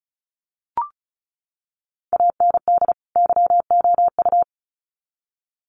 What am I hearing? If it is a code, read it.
Morse code, heard as a keyed tone, about 34 wpm, 700 Hz. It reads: AND YOU
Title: Sets of 2 Words 35wpm